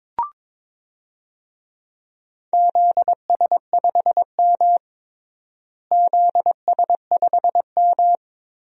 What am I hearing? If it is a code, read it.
Morse code, heard as a keyed tone, about 22 wpm, 700 Hz. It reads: ZS5M ZS5M
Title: DXCC-100 Call Signs - 4 Characters - Repeated 2x - 22wpm